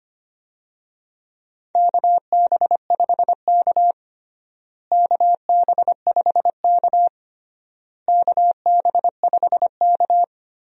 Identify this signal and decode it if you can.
Morse code, heard as a keyed tone, about 25 wpm, 700 Hz. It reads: KB5K KB5K KB5K
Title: USA Call Signs - 4 Characters - Repeated 3x - 25wpm